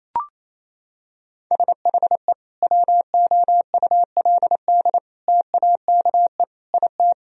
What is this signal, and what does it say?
Morse code, heard as a keyed tone, about 28 wpm, 700 Hz. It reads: SHE WOULD TAKE IT
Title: Sentences from Top 1000 Words - Review Part 1 - 28wpm